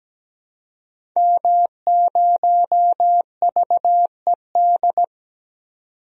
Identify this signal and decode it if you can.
Morse code, heard as a keyed tone, about 17 wpm, 700 Hz. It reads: M0VED
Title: DXCC-100 Call Signs - 5 Characters - 17wpm